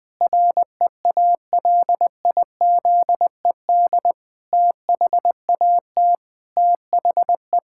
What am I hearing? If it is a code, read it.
Morse code, heard as a keyed tone, about 20 wpm, 700 Hz. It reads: REALIZED THAT THE
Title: Pangram Challenge v2 - 20wpm